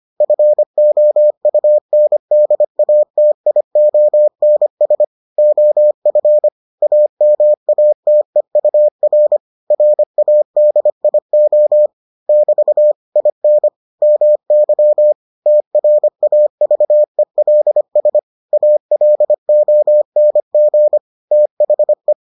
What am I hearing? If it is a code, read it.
Morse code, heard as a keyed tone, about 25 wpm, 600 Hz. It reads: FOUNDATIONS OF AMATEUR RADIO <BT> IN MY TRAVELS ALONG THE